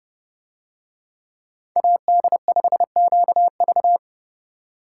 Morse code, heard as a keyed tone, about 31 wpm, 700 Hz. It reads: AD5QV